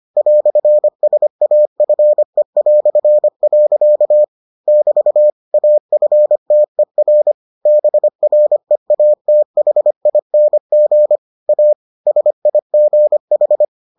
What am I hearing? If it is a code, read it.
Morse code, heard as a keyed tone, about 25 wpm, 600 Hz. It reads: "SAFE". <BT> AFTER BREATHING A SIGH